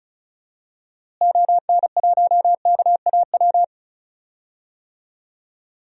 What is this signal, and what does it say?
Morse code, heard as a keyed tone, about 34 wpm, 700 Hz. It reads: ON1KAW
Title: DXCC-100 Call Signs - 6 Characters - Repeated 3x - 35wpm